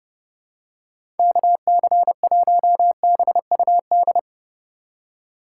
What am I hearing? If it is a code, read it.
Morse code, heard as a keyed tone, about 30 wpm, 700 Hz. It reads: KC1BUD
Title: USA Call Signs - 6 Characters - 4x Character Spacing - 30wpm